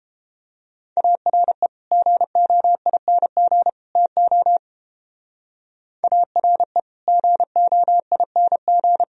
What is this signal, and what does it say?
Morse code, heard as a keyed tone, about 31 wpm, 700 Hz. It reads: ARE GOING TO ARE GOING